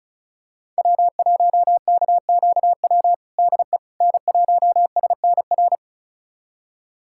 Morse code, heard as a keyed tone, about 34 wpm, 700 Hz. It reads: W1KQW DE N1SNR
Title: USA Callsign DE Callsign - 5 Characters - 35wpm